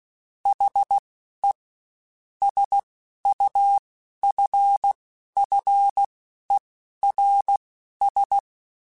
Morse code, heard as a keyed tone, about 16 wpm, spaced as for 11 wpm, 800 Hz. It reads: HE SUFFERS
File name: Quotes Spaced / Quotes Spaced-mp3s / HoofInMouth sp @15